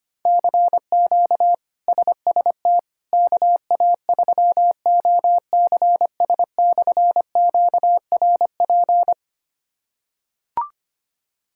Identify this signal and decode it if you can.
Morse code, heard as a keyed tone, about 25 wpm, 700 Hz. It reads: CQ SST KA3OCS/QRP
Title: K1USN SST QSO Practice - 25wpm